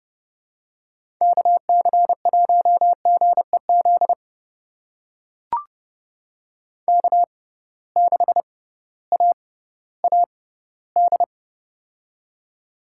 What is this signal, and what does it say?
Morse code, heard as a keyed tone, about 31 wpm, spaced as for 16 wpm, 700 Hz. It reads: KC1GEZ K6AAD
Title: USA Call Signs - 6x Character Spacing - 30wpm